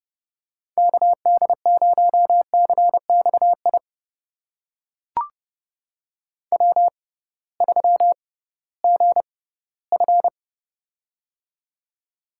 Morse code, heard as a keyed tone, about 30 wpm, spaced as for 18 wpm, 700 Hz. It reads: KD0CXI W3GF
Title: USA Call Signs - 6x Character Spacing - 30wpm